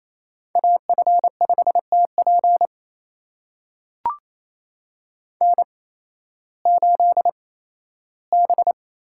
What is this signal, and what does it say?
Morse code, heard as a keyed tone, about 28 wpm, spaced as for 15 wpm, 700 Hz. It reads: AF5TP N8B
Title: USA Call Signs - 5 Characters - 8x Character Spacing - 28wpm